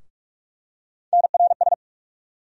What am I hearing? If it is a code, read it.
Morse code, heard as a keyed tone, about 47 wpm, 700 Hz. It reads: NRS
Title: USA Call Sign Postfixes - 3 Letters 45wpm